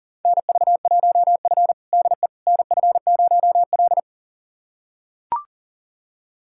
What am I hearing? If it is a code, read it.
Morse code, heard as a keyed tone, about 40 wpm, 700 Hz. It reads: NV1F DE NF0L
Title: USA Callsign DE Callsign - 4 Characters - 40wpm